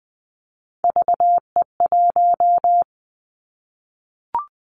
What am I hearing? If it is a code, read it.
Morse code, heard as a keyed tone, about 20 wpm, 700 Hz. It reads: VE1
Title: Callsign Prefixes for Top 100 DXCC - 20wpm